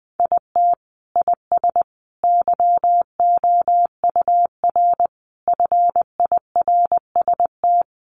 Morse code, heard as a keyed tone, about 19 wpm, 700 Hz. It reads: IT IS YOUR FIRST
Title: Sentences from Top 1100 Words 20wpm